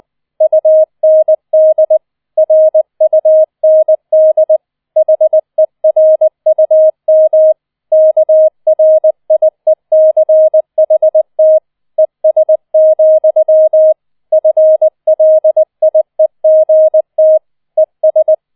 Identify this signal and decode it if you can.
Morse code, heard as a keyed tone, about 20 wpm, 630 Hz. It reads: UND RUND HERUM KRIECHT ES, FLIEGT ES
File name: 20wpm